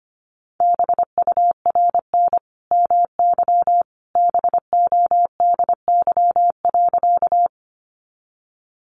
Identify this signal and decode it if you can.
Morse code, heard as a keyed tone, about 25 wpm, 700 Hz. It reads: BURN MY BODY.
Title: Sentences from Top 1300 Words 25wpm